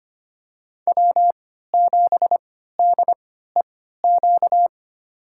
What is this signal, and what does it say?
Morse code, heard as a keyed tone, about 25 wpm, spaced as for 14 wpm, 700 Hz. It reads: W7DEQ
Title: USA Call Signs - 5 Characters - 3x Character Spacing - 25wpm